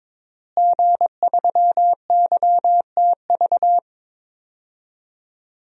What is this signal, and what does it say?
Morse code, heard as a keyed tone, about 22 wpm, 700 Hz. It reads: G3YTV
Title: DXCC-100 Call Signs - 5 Characters - Repeated 3x - 22wpm